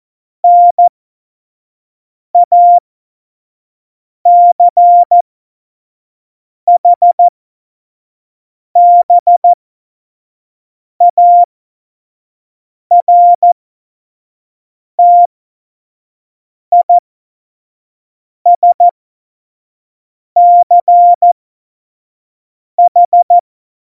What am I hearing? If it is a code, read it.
Morse code, heard as a keyed tone, about 14 wpm, spaced as for 5 wpm, 700 Hz. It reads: NACHBARTISCH
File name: Morsekurs / Witze / Witz 26 Mai 2020, 25 F70